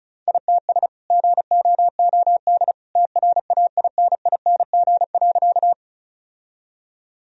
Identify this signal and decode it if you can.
Morse code, heard as a keyed tone, about 34 wpm, 700 Hz. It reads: ITS GOOD TRAINING.
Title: Sentences from Top 1000 Words - Review Part 4 - 35wpm